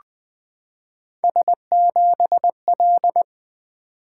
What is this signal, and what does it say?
Morse code, heard as a keyed tone, about 20 wpm, 700 Hz. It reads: S7L
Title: Three Letter-Number Combinations - 20wpm